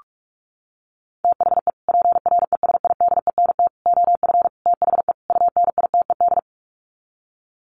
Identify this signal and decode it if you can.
Morse code, heard as a keyed tone, about 44 wpm, 700 Hz. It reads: THE PRESIDENT OF THE UNITED
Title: Sets of 5 Words 45wpm